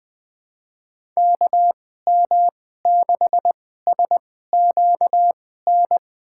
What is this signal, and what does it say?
Morse code, heard as a keyed tone, about 21 wpm, spaced as for 15 wpm, 700 Hz. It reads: KM6SQN
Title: USA Call Signs - 6 Characters - 2x Character Spacing - 20wpm